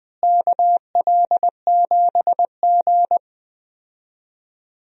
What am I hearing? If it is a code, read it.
Morse code, heard as a keyed tone, about 21 wpm, 700 Hz. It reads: KL7G
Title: DXCC-100 Call Signs - 4 Characters - Repeated 3x - 20wpm